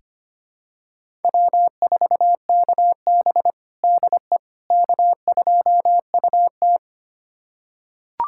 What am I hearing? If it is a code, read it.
Morse code, heard as a keyed tone, about 25 wpm, 700 Hz. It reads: W4KB DE K2UT
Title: USA Callsign DE Callsign - 4 Characters - 25wpm